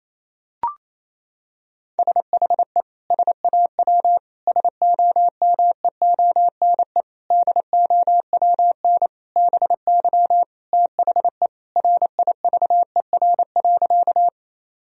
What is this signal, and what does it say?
Morse code, heard as a keyed tone, about 28 wpm, 700 Hz. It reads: SHE SAW SOMEONE DOWN BY THE RIVER.
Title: Sentences from Top 1000 Words - Review Part 4 - 28wpm